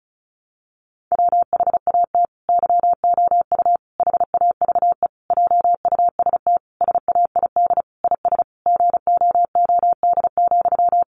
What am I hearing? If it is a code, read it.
Morse code, heard as a keyed tone, about 34 wpm, 700 Hz. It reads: WHAT YOU HAVE JUST SAID IS GOOD,
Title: Sentences from Top 200 Words 35wpm